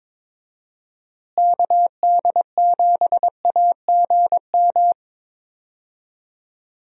Morse code, heard as a keyed tone, about 22 wpm, 700 Hz. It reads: KD7AGM
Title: USA Call Signs - 6 Characters - Repeated 2x - 22wpm